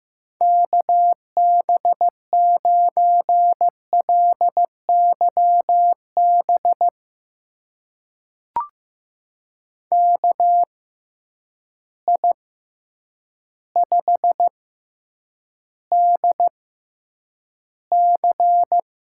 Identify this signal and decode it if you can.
Morse code, heard as a keyed tone, about 15 wpm, spaced as for 8 wpm, 700 Hz. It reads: KB9LYB KI5DC
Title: USA Call Signs - 6x Character Spacing - 15wpm